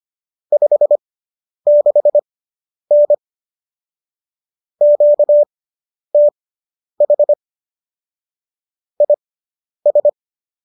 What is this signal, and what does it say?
Morse code, heard as a keyed tone, about 25 wpm, spaced as for 10 wpm, 590 Hz. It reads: 56N QTH IS